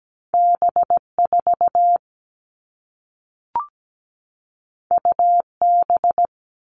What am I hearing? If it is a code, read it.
Morse code, heard as a keyed tone, about 17 wpm, 700 Hz. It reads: B4 UB
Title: All Cut Numbers - 2-Digit 17wpm